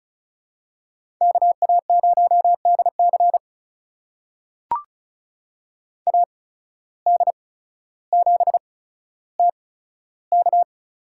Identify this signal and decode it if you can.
Morse code, heard as a keyed tone, about 36 wpm, spaced as for 15 wpm, 700 Hz. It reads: KA0DC AD7TK
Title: USA Call Signs - 5 Characters - 8x Character Spacing - 35wpm